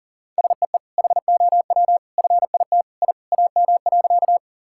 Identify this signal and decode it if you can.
Morse code, heard as a keyed tone, about 37 wpm, 700 Hz. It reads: SEE HOW FIT I AM.
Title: Sentences from Top 1000 Words - Review Part 5 - 40wpm